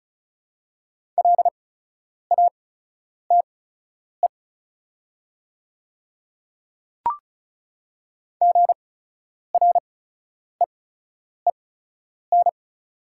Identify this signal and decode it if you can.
Morse code, heard as a keyed tone, about 34 wpm, spaced as for 10 wpm, 690 Hz. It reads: LATE GREEN
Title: 8x Character Spacing - Top 500 Words - 35wpm